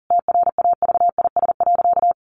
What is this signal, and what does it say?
Morse code, heard as a keyed tone, about 40 wpm, 700 Hz. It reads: TRAVIS.
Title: A Sound Of Thunder 40wpm